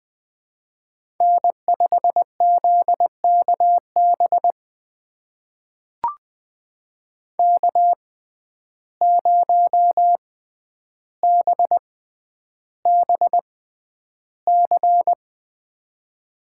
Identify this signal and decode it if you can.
Morse code, heard as a keyed tone, about 20 wpm, spaced as for 10 wpm, 700 Hz. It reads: N5ZKB K0BBC
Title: USA Call Signs - 5 Characters - 6x Character Spacing - 20wpm